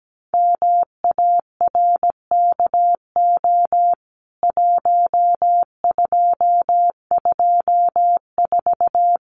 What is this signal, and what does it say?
Morse code, heard as a keyed tone, about 17 wpm, 700 Hz. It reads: MARKO 1224
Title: CWT Exchange 17wpm